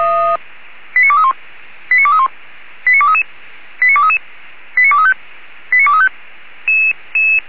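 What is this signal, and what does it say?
Five-tone selective call (ZVEI), heard as quick runs of five tones, tones of about 80 ms. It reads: 89231 89231 89233 89233 89235 89235
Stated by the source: ZVEI five-tone selective call 89240, received by SDR.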